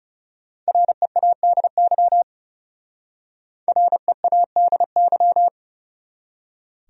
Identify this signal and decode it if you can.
Morse code, heard as a keyed tone, about 34 wpm, 700 Hz. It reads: READY READY
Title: Speed-Racing Top 500 Words 30wpm